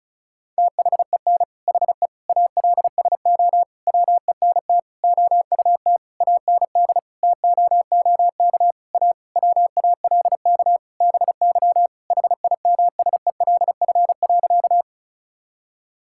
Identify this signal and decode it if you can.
Morse code, heard as a keyed tone, about 36 wpm, 700 Hz. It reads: THEN HE ALSO WENT OUT AND TOOK A WALK BY HIMSELF.
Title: Sentences from Top 1600 Words 35wpm